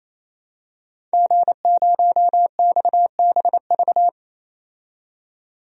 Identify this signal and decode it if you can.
Morse code, heard as a keyed tone, about 27 wpm, 700 Hz. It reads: G0XBV
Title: DXCC-100 Call Signs - 5 Characters - 28wpm